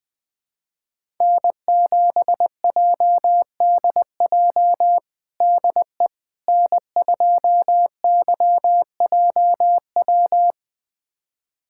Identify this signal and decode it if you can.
Morse code, heard as a keyed tone, about 20 wpm, 700 Hz. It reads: N7JDJ DE N2YJW
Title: USA Callsign DE Callsign 20wpm